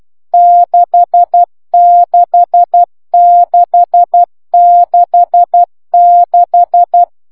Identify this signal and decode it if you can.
Morse code, heard as a keyed tone, about 12 wpm, 1700 Hz, with noise. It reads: DE6666